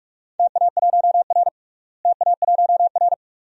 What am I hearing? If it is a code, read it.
Morse code, heard as a keyed tone, about 46 wpm, 700 Hz. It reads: TA1R TA1R
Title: DXCC-100 Call Signs - 4 Characters - Repeated 2x - 45wpm